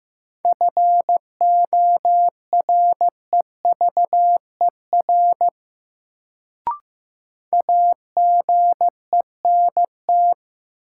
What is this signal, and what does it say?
Morse code, heard as a keyed tone, about 15 wpm, 700 Hz. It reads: FOREVER AGENT
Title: Sentences from Top 1600 Words - New Word Drill Alt - 15wpm